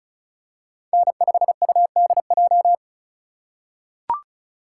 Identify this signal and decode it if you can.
Morse code, heard as a keyed tone, about 36 wpm, 700 Hz. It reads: N5UDJ
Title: USA Call Signs - 5 Characters - 3x Character Spacing - 35wpm